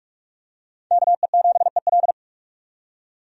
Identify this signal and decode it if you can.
Morse code, heard as a keyed tone, about 46 wpm, 700 Hz. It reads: KE7EL
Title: USA Call Signs - 5 Characters - 45wpm